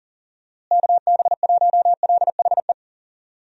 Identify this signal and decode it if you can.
Morse code, heard as a keyed tone, about 40 wpm, 700 Hz. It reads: KB1LHE
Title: USA Call Signs - 6 Characters - Repeated 2x - 40wpm